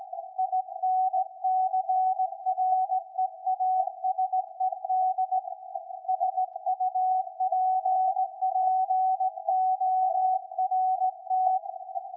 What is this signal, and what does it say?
Morse code, heard as a keyed tone, about 17 wpm, 750 Hz. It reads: INCREASED SUPPORT
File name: iz7slz / VLF / SAQ 13-02-2023